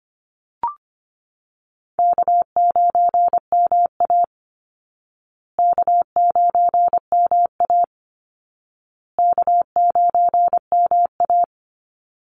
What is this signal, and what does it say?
Morse code, heard as a keyed tone, about 25 wpm, 700 Hz. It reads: K9MA K9MA K9MA
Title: Easy Call Signs Repeated 3x - 25wpm